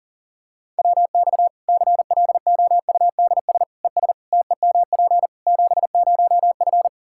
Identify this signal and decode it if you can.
Morse code, heard as a keyed tone, about 37 wpm, 700 Hz. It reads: WX CLOUDS ES TEMP 70F
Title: Practice QSO - Exchange 2 - Info - 40wpm